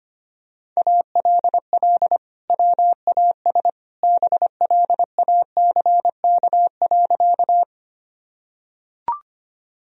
Morse code, heard as a keyed tone, about 25 wpm, 700 Hz. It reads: ALL WAS BLACK.
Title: Sentences from Top 1000 Words - Review Part 2 - 25wpm